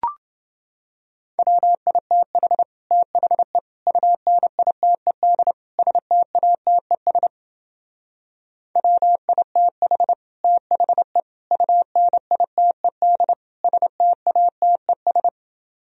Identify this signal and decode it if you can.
Morse code, heard as a keyed tone, about 29 wpm, 700 Hz. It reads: WITH THE UNITED STATES WITH THE UNITED STATES